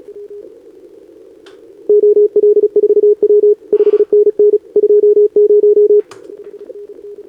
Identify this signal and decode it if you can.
Morse code, heard as a keyed tone, about 31 wpm, 410 Hz. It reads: OL4W 5NN 20
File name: CQWWCW2025 / NedeleOdpol 20m Scan